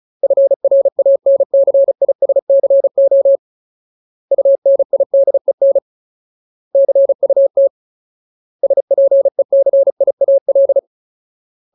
Morse code, heard as a keyed tone, about 34 wpm, 550 Hz. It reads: FRANCISCO UNIDEN CUT SPECIAL
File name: Ultimate Word List / 35WPM Ultimate